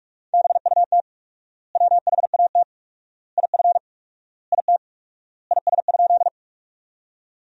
Morse code, heard as a keyed tone, about 46 wpm, 700 Hz. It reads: BUT WHAT IF IT IS?
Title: 4x Word-Spacing - Sentences from Top 100 Words - 45wpm